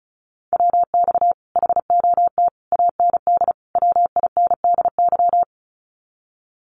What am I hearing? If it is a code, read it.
Morse code, heard as a keyed tone, about 34 wpm, 700 Hz. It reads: WX HOT AND WINDY
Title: Common QSO Phrases Encore 35wpm